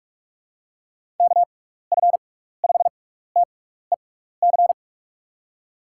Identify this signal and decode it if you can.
Morse code, heard as a keyed tone, about 42 wpm, 700 Hz. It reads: K F 5 T E C
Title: USA Call Signs - 6 Characters - 6x Character Spacing - 45wpm